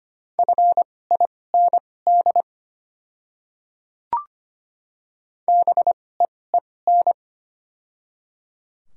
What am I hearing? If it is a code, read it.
Morse code, heard as a keyed tone, about 25 wpm, spaced as for 18 wpm, 700 Hz. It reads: FIND BEEN
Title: 2x Character Spacing - Top 200 Words - 25wpm